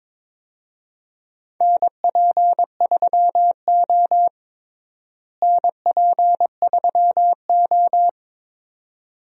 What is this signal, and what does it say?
Morse code, heard as a keyed tone, about 22 wpm, 700 Hz. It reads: NP3O NP3O
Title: USA Call Signs - 4 Characters - Repeated 2x - 22wpm